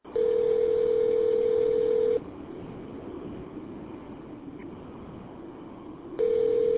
Ringback tone (North America), 440 + 480 Hz.